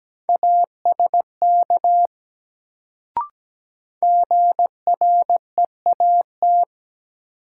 Morse code, heard as a keyed tone, about 17 wpm, 700 Hz. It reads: ASK GREAT
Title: Sentences from Top 200 Words - New Word Drill - 17wpm